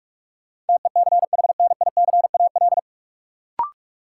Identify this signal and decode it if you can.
Morse code, heard as a keyed tone, about 46 wpm, 700 Hz. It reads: TECHNICAL
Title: Sentences from Top 1100 Words - New Word Drill - 45wpm